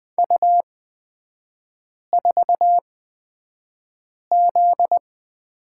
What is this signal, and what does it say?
Morse code, heard as a keyed tone, about 20 wpm, spaced as for 5 wpm, 700 Hz. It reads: U4Z
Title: 243 Lesson 39e - Letter z - Three Characters 20wpm